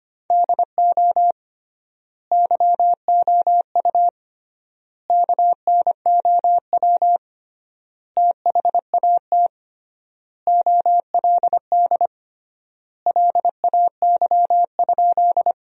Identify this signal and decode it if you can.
Morse code, heard as a keyed tone, about 25 wpm, 700 Hz. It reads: DO YOU KNOW THAT OLD LAY?
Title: 3x Word-Spacing - Sentences from Top 400 Words - 25wpm